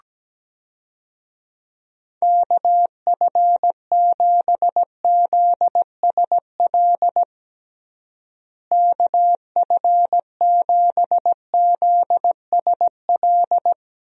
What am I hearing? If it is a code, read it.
Morse code, heard as a keyed tone, about 17 wpm, 700 Hz. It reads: KF7ZSL KF7ZSL
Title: USA Call Signs - 6 Characters - Repeated 2x - 17wpm